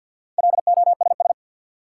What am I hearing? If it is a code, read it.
Morse code, heard as a keyed tone, about 48 wpm, 700 Hz. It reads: LOSS
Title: Sentences from Top 1400 Words - New Word Drill Alt - 50wpm